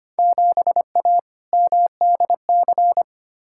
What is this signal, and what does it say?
Morse code, heard as a keyed tone, about 25 wpm, 700 Hz. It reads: 7A MDC
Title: Field Day Exchanges Encore - 25wpm